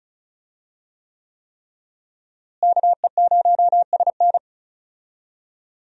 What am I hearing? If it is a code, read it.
Morse code, heard as a keyed tone, about 34 wpm, 700 Hz. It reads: KE0SN
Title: USA Call Signs - 5 Characters - Repeated 3x - 35wpm